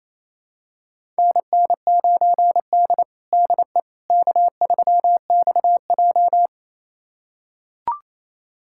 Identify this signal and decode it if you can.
Morse code, heard as a keyed tone, about 28 wpm, 700 Hz. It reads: NN9D DE K3XJ
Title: USA Callsign DE Callsign - 4 Characters - 28wpm